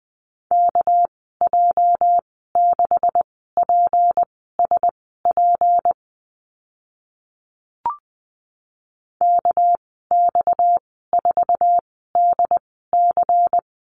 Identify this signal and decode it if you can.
Morse code, heard as a keyed tone, about 20 wpm, spaced as for 15 wpm, 700 Hz. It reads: KJ6PSP KX4DC
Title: 245 Lesson 39g - Letter z - Call Signs 20wpm